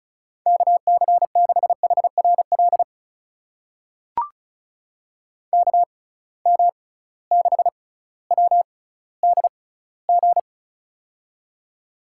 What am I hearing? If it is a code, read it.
Morse code, heard as a keyed tone, about 36 wpm, spaced as for 18 wpm, 700 Hz. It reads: KC6HRL KM6WDG
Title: USA Call Signs - 6 Characters - 6x Character Spacing - 35wpm